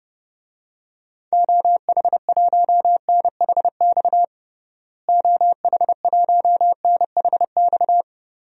Morse code, heard as a keyed tone, about 32 wpm, 700 Hz. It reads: OH1NHX OH1NHX
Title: DXCC-100 Call Signs - 6 Characters - Repeated 3x - 30wpm